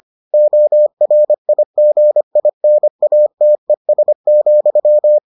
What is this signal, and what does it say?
Morse code, heard as a keyed tone, about 25 wpm, 600 Hz. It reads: ORIGINATES,